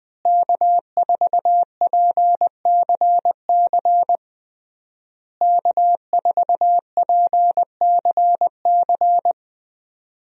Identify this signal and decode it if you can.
Morse code, heard as a keyed tone, about 21 wpm, 700 Hz. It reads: K4PCC K4PCC
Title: USA Call Signs - 5 Characters - Repeated 3x - 20wpm